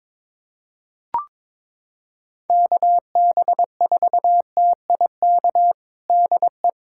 Morse code, heard as a keyed tone, about 22 wpm, 700 Hz. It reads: KB4TIK DE
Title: USA Callsign DE Callsign - 6 Characters - 22wpm